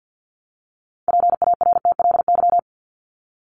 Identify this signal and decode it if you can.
Morse code, heard as a keyed tone, about 50 wpm, 700 Hz. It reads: PARTLY
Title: LY Suffix - Top 100 Words with Base Words 50wpm